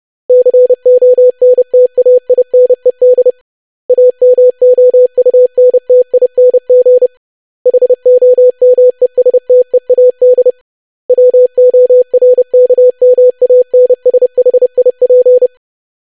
Morse code, heard as a keyed tone, about 30 wpm, 500 Hz. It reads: CONTAINED AMOUNTING HOMESTEAD WORKMANSHIP
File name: lesson / web / longwords-25wpm